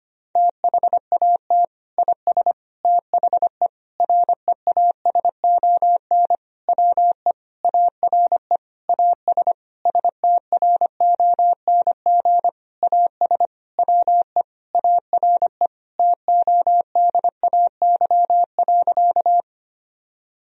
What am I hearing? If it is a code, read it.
Morse code, heard as a keyed tone, about 25 wpm, 690 Hz. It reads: THAT IS THE REASON WE ARE AS STRONG AS WE ARE TODAY.
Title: Sentences from Top 1000 Words - Review Part 4 - 25wpm